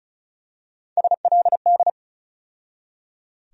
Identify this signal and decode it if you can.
Morse code, heard as a keyed tone, about 36 wpm, 700 Hz. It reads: SLD
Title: USA Call Sign Postfixes - 3 Letters 35wpm